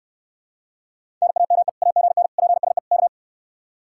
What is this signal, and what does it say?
Morse code, heard as a keyed tone, about 69 wpm, 700 Hz. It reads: NAME AGN PSE K